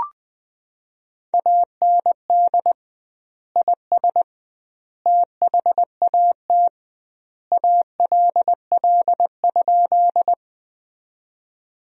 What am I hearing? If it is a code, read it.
Morse code, heard as a keyed tone, about 20 wpm, 700 Hz. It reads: AND IS THAT ALL?
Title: 2x Word-Spacing - Sentences from Top 100 Words - 20wpm